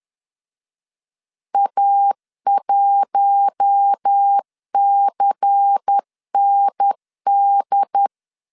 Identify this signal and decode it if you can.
Morse code, heard as a keyed tone, about 10 wpm, 800 Hz. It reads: A1CND
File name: LEZIONI / 6 PKX0 / 6 PKX 10wpm 2426